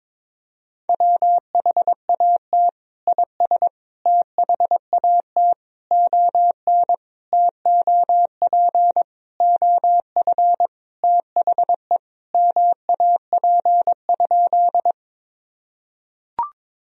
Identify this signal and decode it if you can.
Morse code, heard as a keyed tone, about 22 wpm, 700 Hz. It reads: WHAT IS THAT ON TOP OF THE MAP?
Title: Sentences from Top 1000 Words - Review Part 2 - 22wpm